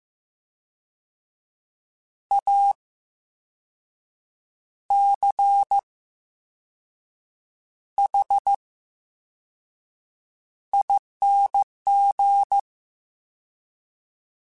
Morse code, heard as a keyed tone, about 15 wpm, spaced as for 5 wpm, 800 Hz. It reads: ACHING
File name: Special INGs / Special INGs-mp3s / Slow ing @13